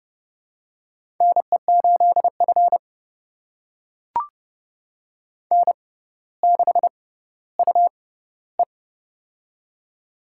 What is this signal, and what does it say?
Morse code, heard as a keyed tone, about 30 wpm, spaced as for 15 wpm, 700 Hz. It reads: NE8F N6UE
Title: USA Call Signs - 4 Characters - 6x Character Spacing - 30wpm